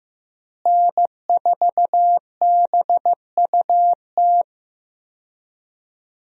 Morse code, heard as a keyed tone, about 15 wpm, 700 Hz. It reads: N4BUT
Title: USA Call Signs - 5 Characters - Repeated 3x - 15wpm